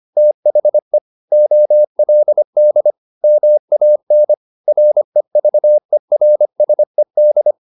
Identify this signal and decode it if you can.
Morse code, heard as a keyed tone, about 25 wpm, 600 Hz. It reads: THE OLD MAN REVERSED